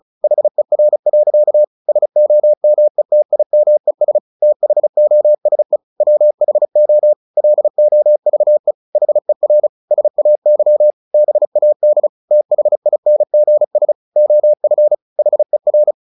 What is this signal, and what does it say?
Morse code, heard as a keyed tone, about 35 wpm, 600 Hz. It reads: HER. SOMETIMES THOSE WHO LOVE HER SAY BAD THINGS OF HER
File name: the old man and the sea / the old man and the sea 1 35wpm